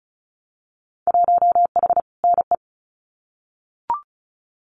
Morse code, heard as a keyed tone, about 34 wpm, 700 Hz. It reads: 1H NE
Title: Winter Field Day Exchanges Encore 35wpm